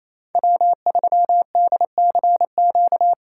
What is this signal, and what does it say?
Morse code, heard as a keyed tone, about 34 wpm, 700 Hz. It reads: W3DCQ
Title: Speed Racing - USA 1x3 Call Signs 28wpm